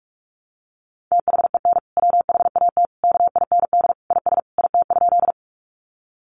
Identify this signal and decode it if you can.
Morse code, heard as a keyed tone, about 44 wpm, 700 Hz. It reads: THEN WHAT KIND IS IT?
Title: Sentences from Top 200 Words 45wpm